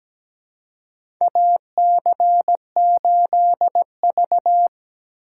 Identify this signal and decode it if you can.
Morse code, heard as a keyed tone, about 17 wpm, 700 Hz. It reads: AC8V
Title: USA Call Signs - 3x Character Spacing - 17wpm